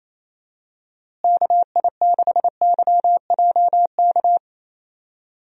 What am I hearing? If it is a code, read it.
Morse code, heard as a keyed tone, about 27 wpm, 700 Hz. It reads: KI6YJK
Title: USA Call Signs - 6 Characters - Repeated 3x - 28wpm